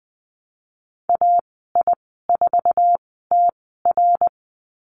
Morse code, heard as a keyed tone, about 20 wpm, spaced as for 14 wpm, 700 Hz. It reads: AI4TR
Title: 055 Lesson 10g - Letter r - Call Signs 20wpm